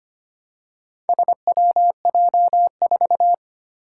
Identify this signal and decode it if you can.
Morse code, heard as a keyed tone, about 25 wpm, 700 Hz. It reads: SWJ4
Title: Four Challenging Letter-Number Combinations 25wpm